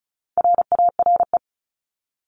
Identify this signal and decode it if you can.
Morse code, heard as a keyed tone, about 34 wpm, 700 Hz. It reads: RARE
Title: Top 1001-1250 Words 35wpm